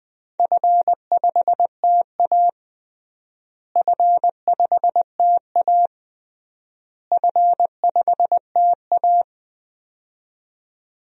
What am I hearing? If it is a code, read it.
Morse code, heard as a keyed tone, about 20 wpm, 700 Hz. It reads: F5TA F5TA F5TA
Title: DXCC-100 Call Signs - 4 Characters - Repeated 3x - 20wpm